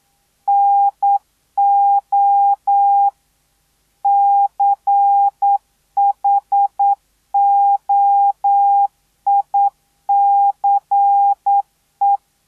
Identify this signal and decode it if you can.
Morse code, heard as a keyed tone, about 9 wpm, 800 Hz. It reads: NO CHOICE